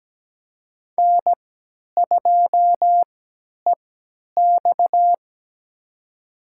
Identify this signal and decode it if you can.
Morse code, heard as a keyed tone, about 17 wpm, spaced as for 10 wpm, 700 Hz. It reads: N2EX
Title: USA Call Signs - 4 Characters - 3x Character Spacing - 17wpm